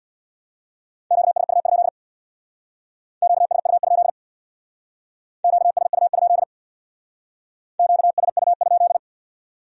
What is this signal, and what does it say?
Morse code, heard as a keyed tone, about 62 wpm, 700 Hz. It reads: QSU? QSU? QSU? QSU?
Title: Speed-Racing Q-Codes - 50wpm